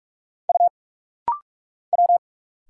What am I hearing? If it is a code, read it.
Morse code, heard as a keyed tone, about 44 wpm, 700 Hz. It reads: U W
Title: Single Letters - Rapid-Fire 45wpm